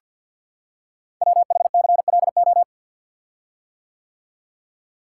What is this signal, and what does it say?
Morse code, heard as a keyed tone, about 51 wpm, 700 Hz. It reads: WHCLO
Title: Code Groups - 5 Letter:Numbers - 50wpm